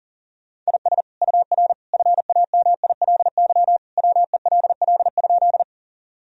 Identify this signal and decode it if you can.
Morse code, heard as a keyed tone, about 40 wpm, 700 Hz. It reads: IS UR FAMILY WELL?
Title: Sentences from Top 400 Words and Abbreviations 40wpm